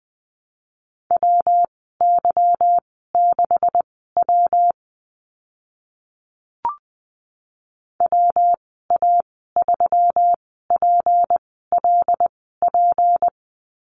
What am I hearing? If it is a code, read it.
Morse code, heard as a keyed tone, about 20 wpm, spaced as for 14 wpm, 700 Hz. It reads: WY6W WA3PLP
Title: 161 Lesson 26g - Letter g - Call Signs 20wpm